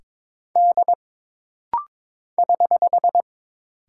Morse code, heard as a keyed tone, about 22 wpm, 700 Hz. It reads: D <HH>
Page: This rapid-fire timing is geared to push your ICR (Instant Character Recognition) skills.